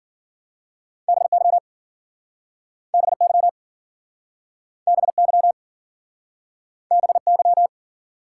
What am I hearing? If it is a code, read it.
Morse code, heard as a keyed tone, about 52 wpm, 700 Hz. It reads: BY BY BY BY